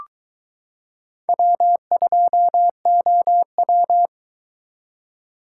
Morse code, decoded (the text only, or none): W2OW